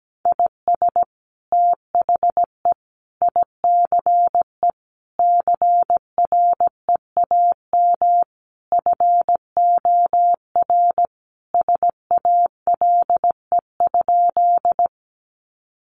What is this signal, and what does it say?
Morse code, heard as a keyed tone, about 17 wpm, 700 Hz. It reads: IS THE ICE CREAM FOR SALE?
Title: Sentences from Top 1500 Words 17wpm